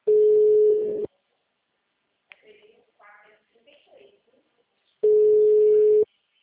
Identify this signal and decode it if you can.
Ringback tone (Europe, 425 Hz), with speech.